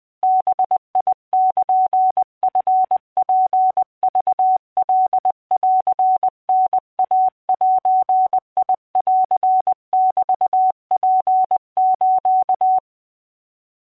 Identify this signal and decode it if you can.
Morse code (Japanese Wabun code), heard as a keyed tone, about 20 wpm, 750 Hz. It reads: バルチツクカンタイゼンメツス